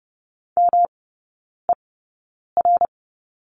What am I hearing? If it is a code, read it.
Morse code, heard as a keyed tone, about 30 wpm, spaced as for 9 wpm, 700 Hz. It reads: MER
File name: Lesson 18e - Letter m - Three Characters 30wpm